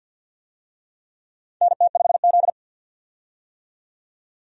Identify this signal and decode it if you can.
Morse code, heard as a keyed tone, about 47 wpm, 700 Hz. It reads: NT5Z
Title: USA Call Signs - 4 Characters - 50wpm